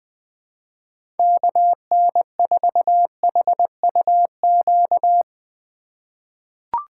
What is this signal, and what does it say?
Morse code, heard as a keyed tone, about 20 wpm, 700 Hz. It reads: KN4HUQ
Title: USA Call Signs - 6 Characters - 2x Character Spacing - 20wpm